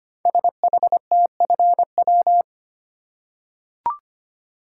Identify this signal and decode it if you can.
Morse code, heard as a keyed tone, about 25 wpm, 700 Hz. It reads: SHTFW
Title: Code Groups - 5 Letters - 25wpm